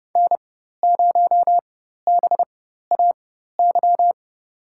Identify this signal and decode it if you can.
Morse code, heard as a keyed tone, about 30 wpm, spaced as for 14 wpm, 700 Hz. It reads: N0BAY
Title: USA Call Signs - 5 Characters - 4x Character Spacing - 30wpm